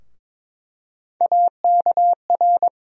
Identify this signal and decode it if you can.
Morse code, heard as a keyed tone, about 22 wpm, 700 Hz. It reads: AKR
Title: USA Call Sign Postfixes - 3 Letters 22wpm